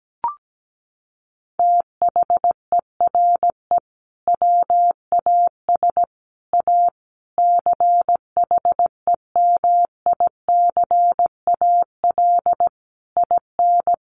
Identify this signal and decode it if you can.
Morse code, heard as a keyed tone, about 17 wpm, 700 Hz. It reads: THERE WAS A CHEMICAL IN
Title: Sentences from Top 1200 Words 17wpm